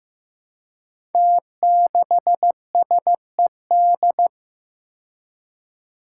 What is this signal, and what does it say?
Morse code, heard as a keyed tone, about 15 wpm, 700 Hz. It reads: T6SED
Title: Five Challenging Letter-Number Combinations 15wpm